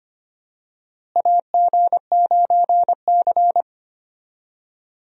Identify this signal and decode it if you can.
Morse code, heard as a keyed tone, about 25 wpm, 700 Hz. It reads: AG9C
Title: USA Call Signs - 4 Characters - Repeated 2x - 25wpm